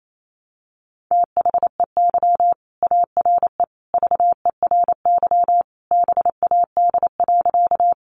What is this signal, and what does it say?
Morse code, heard as a keyed tone, about 28 wpm, 700 Hz. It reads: THEY ARE VERY BAD.
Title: Sentences from Top 700 Words 28wpm